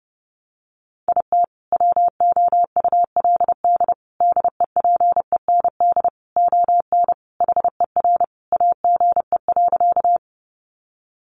Morse code, heard as a keyed tone, about 30 wpm, 700 Hz. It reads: IT WOULD DEPEND ON HER AGE.
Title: Sentences from Top 1300 Words 30wpm